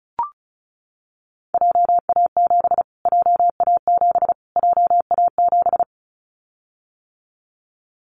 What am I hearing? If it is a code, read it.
Morse code, heard as a keyed tone, about 34 wpm, 700 Hz. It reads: JA7 JA7 JA7
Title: Callsign Prefixes for Top 25 DXCC - Repeated 3x - 35wpm